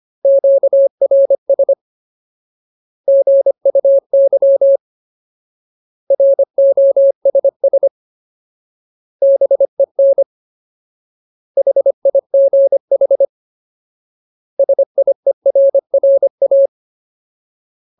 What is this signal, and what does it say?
Morse code, heard as a keyed tone, about 25 wpm, 550 Hz. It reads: QRS GUY ROSS BEN HIGH SIERRA
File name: Ultimate Word List / 25WPM Ultimate